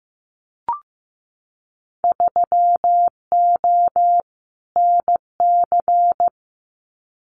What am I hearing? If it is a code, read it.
Morse code, heard as a keyed tone, about 15 wpm, 700 Hz. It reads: 3O NC
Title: Winter Field Day Exchanges 15wpm